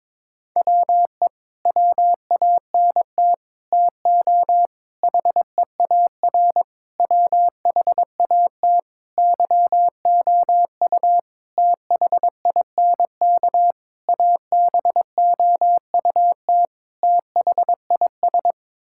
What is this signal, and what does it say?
Morse code, heard as a keyed tone, about 21 wpm, 700 Hz. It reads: WE WANT TO HEAR WHAT YOU THINK ABOUT THIS
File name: Sets of 9 Words - 22wpm